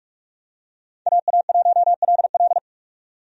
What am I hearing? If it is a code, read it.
Morse code, heard as a keyed tone, about 41 wpm, 700 Hz. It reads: AA1LL
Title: USA Call Signs - 5 Characters - 45wpm